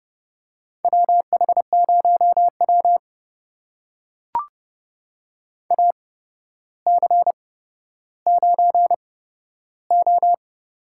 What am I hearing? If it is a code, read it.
Morse code, heard as a keyed tone, about 30 wpm, spaced as for 13 wpm, 700 Hz. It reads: WH0W AC9O
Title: USA Call Signs - 4 Characters - 8x Character Spacing - 30wpm